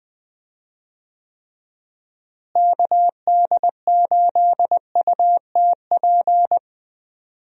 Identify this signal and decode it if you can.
Morse code, heard as a keyed tone, about 20 wpm, 700 Hz. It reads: KD8UTP
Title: USA Call Signs - 6 Characters - Repeated 2x - 20wpm